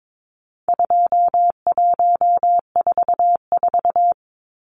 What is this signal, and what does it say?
Morse code, heard as a keyed tone, about 22 wpm, 700 Hz. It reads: 2144
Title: Numbers - 4-Digit 22wpm